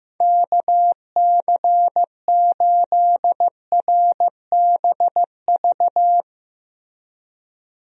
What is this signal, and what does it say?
Morse code, heard as a keyed tone, about 15 wpm, 700 Hz. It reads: KC8RBV
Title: USA Call Signs - 6 Characters - Repeated 2x - 15wpm